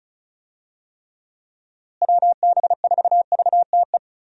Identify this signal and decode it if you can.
Morse code, heard as a keyed tone, about 34 wpm, 700 Hz. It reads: WB4VTE
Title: USA Call Signs - 6 Characters - Repeated 3x - 35wpm